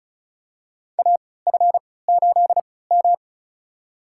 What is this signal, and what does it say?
Morse code, heard as a keyed tone, about 36 wpm, spaced as for 20 wpm, 700 Hz. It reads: AF8M
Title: USA Call Signs - 4 Characters - 3x Character Spacing - 35wpm